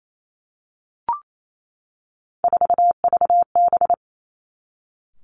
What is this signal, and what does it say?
Morse code, heard as a keyed tone, about 28 wpm, 700 Hz. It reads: 4VB
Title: Three Challenging Letter-Number Combinations 28wpm